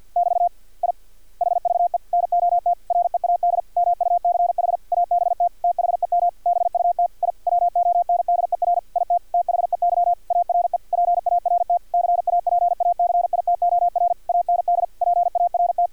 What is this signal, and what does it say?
Morse code, heard as a keyed tone, about 50 wpm, 700 Hz, with radio noise. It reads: <BT> I HVE NOT READ MUCH ABT THEM BUT I WONDER IT THEY ARE PART CAPACITOR AND PART
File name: archived / CRSnet